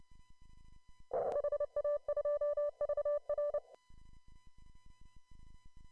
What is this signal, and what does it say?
Morse code, heard as a keyed tone, about 29 wpm, 1750 Hz, with noise. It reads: HA2VR